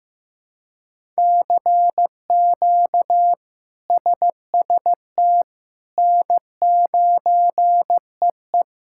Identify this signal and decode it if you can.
Morse code, heard as a keyed tone, about 15 wpm, 700 Hz. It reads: CQ SST N9EE
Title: K1USN SST QSO Practice - 15wpm